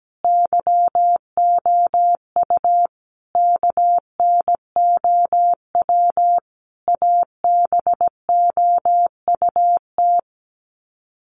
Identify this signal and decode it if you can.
Morse code, heard as a keyed tone, about 17 wpm, 700 Hz. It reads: YOU KNOW ABOUT
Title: Sets of 5 Words Encore 17wpm